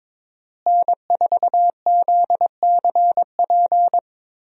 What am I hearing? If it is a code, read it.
Morse code, heard as a keyed tone, about 22 wpm, 700 Hz. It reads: N4ZCP
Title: USA Call Signs - 5 Characters - Repeated 2x - 22wpm